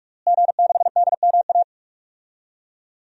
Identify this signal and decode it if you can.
Morse code, heard as a keyed tone, about 41 wpm, 700 Hz. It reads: G6DMA
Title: DXCC-100 Call Signs - 5 Characters - Repeated 3x - 45wpm